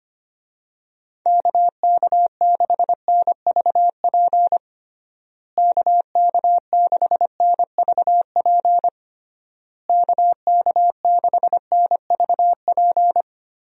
Morse code, heard as a keyed tone, about 25 wpm, 700 Hz. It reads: KK6NVP KK6NVP KK6NVP
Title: USA Call Signs - 6 Characters - Repeated 3x - 25wpm